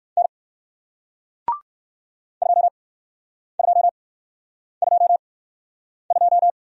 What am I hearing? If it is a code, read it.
Morse code, heard as a keyed tone, about 61 wpm, spaced as for 9 wpm, 700 Hz. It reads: N 2222